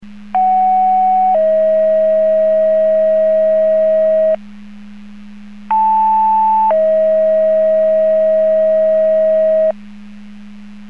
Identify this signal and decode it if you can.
Two-tone page, heard as a one-second tone, then a three-second tone, about 750 then 640 Hz, about 880 then 640 Hz. It is heard from 0.3 to 9.7 s.
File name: Portage 154.385 Paging